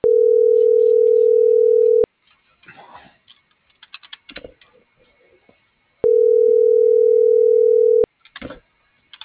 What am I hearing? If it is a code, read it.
Ringback tone (North America), 440 + 480 Hz.